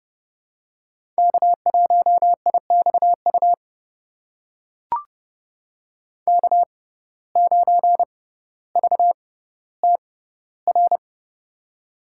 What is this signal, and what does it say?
Morse code, heard as a keyed tone, about 30 wpm, spaced as for 15 wpm, 700 Hz. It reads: K1IXU K9VTR
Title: USA Call Signs - 5 Characters - 6x Character Spacing - 30wpm